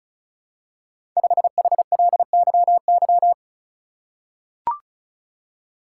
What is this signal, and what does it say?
Morse code, heard as a keyed tone, about 34 wpm, 700 Hz. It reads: 5HLYY
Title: Code Groups - 5 Letter:Numbers - 35wpm